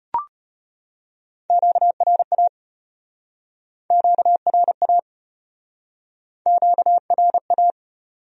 Morse code, heard as a keyed tone, about 34 wpm, 700 Hz. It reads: QRA QRA QRA